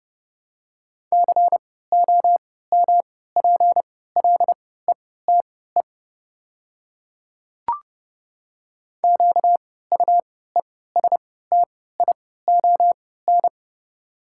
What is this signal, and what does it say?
Morse code, heard as a keyed tone, about 30 wpm, spaced as for 17 wpm, 700 Hz. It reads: COMPLETE QUESTION
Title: 3x Character Spacing - Top 500 Words - 30wpm